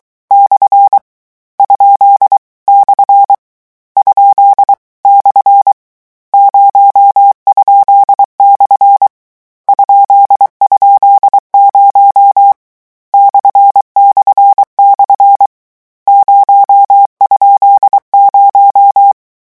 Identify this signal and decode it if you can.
Morse code, heard as a keyed tone, about 23 wpm, 800 Hz. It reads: / ? / ? / 0?/ ??0 /// 0?0
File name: morse mp3 / lecon13-vitesse 22